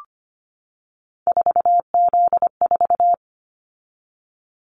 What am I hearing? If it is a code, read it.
Morse code, heard as a keyed tone, about 25 wpm, 700 Hz. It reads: 4Z4
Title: Callsign Prefixes for Top 50 DXCC - 25wpm